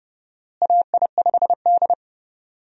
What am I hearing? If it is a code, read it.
Morse code, heard as a keyed tone, about 30 wpm, 700 Hz. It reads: AI5D
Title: USA Call Signs - 4 Characters - Repeated 3x - 30wpm